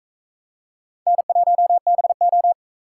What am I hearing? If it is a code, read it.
Morse code, heard as a keyed tone, about 45 wpm, 700 Hz. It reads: N1BO